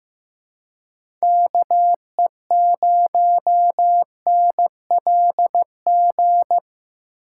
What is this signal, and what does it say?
Morse code, heard as a keyed tone, about 15 wpm, 700 Hz. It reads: KE0NLG
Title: USA Call Signs - 4x Character Spacing - 15wpm